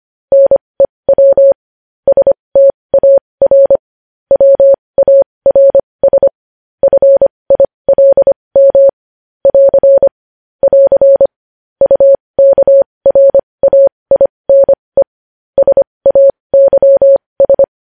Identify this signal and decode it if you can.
Morse code, heard as a keyed tone, about 25 wpm, spaced as for 20 wpm, 550 Hz. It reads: NEW STAR WARS FILM <AR> <AR> UKRAINE SAYS
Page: news-headlines-in-morse-code-at-20-wpm